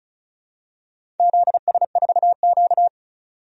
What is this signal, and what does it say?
Morse code, heard as a keyed tone, about 36 wpm, 700 Hz. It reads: ZS4Q
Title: DXCC-100 Call Signs - 4 Characters - 35wpm